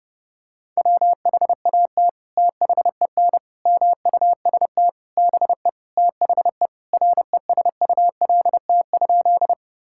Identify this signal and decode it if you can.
Morse code, heard as a keyed tone, about 32 wpm, 700 Hz. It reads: WHAT THEN MUST BE THE RESULT?
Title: Sentences from Top 1000 Words - Review Part 5 - 30wpm